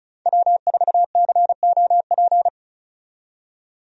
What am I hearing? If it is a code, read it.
Morse code, heard as a keyed tone, about 36 wpm, 700 Hz. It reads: W4COP
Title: USA Call Signs - 5 Characters - 35wpm